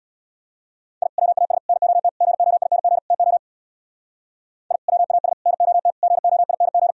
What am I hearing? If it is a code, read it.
Morse code, heard as a keyed tone, about 63 wpm, 700 Hz. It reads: I WAS NOT GOING TO I WAS NOT GOING